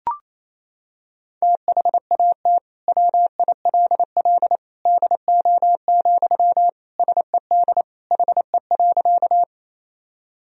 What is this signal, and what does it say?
Morse code, heard as a keyed tone, about 28 wpm, 700 Hz. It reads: THAT WILL DO, SED HE.
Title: Sentences from Top 100 Words and Abbreviations 28wpm